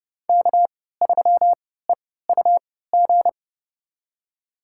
Morse code, heard as a keyed tone, about 32 wpm, spaced as for 17 wpm, 700 Hz. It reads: K3EUG
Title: USA Call Signs - 5 Characters - 3x Character Spacing - 30wpm